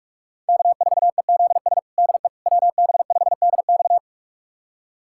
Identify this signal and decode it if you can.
Morse code, heard as a keyed tone, about 46 wpm, 700 Hz. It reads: K4EZS DE WB5DX
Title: USA Callsign DE Callsign - 5 Characters - 45wpm